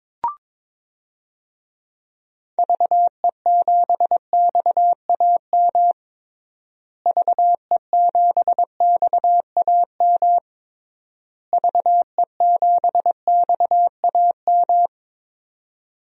Morse code, heard as a keyed tone, about 22 wpm, 700 Hz. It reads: VE7XAM VE7XAM VE7XAM
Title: DXCC-100 Call Signs - 6 Characters - Repeated 3x - 22wpm